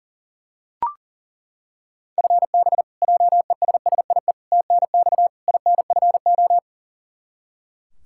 Morse code, heard as a keyed tone, about 40 wpm, 700 Hz. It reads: FB JESSIE TNX INFO
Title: Practice QSO - Exchange 3 - More Info - 40wpm